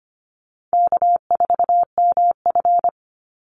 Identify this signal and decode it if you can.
Morse code, heard as a keyed tone, about 25 wpm, 700 Hz. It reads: K4MF
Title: NA SOTA Chaser Callsigns 25wpm